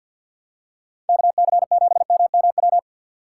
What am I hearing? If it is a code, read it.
Morse code, heard as a keyed tone, about 51 wpm, 700 Hz. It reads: KC7MMW
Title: USA Call Signs - 6 Characters - 3x Character Spacing - 50wpm